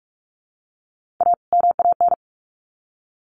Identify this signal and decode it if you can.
Morse code, heard as a keyed tone, about 44 wpm, 700 Hz. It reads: A MAN
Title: Sets of 2 Words 45wpm